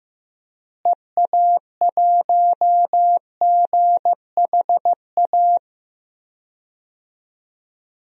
Morse code, heard as a keyed tone, about 15 wpm, 700 Hz. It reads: EA1GHA
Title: DXCC-100 Call Signs - 6 Characters - Repeated 3x - 15wpm